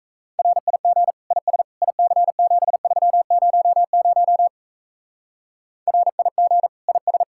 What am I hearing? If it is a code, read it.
Morse code, heard as a keyed tone, about 36 wpm, 700 Hz. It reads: RIG IS IC7300 RIG IS